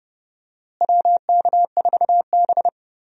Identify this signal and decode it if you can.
Morse code, heard as a keyed tone, about 30 wpm, 700 Hz. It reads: WK4B
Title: USA Call Signs - 4 Characters - Repeated 3x - 30wpm